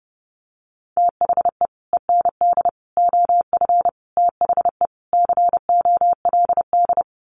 Morse code, heard as a keyed tone, about 30 wpm, 700 Hz. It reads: THE END OF THE COLD
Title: Sets of 5 Words 30wpm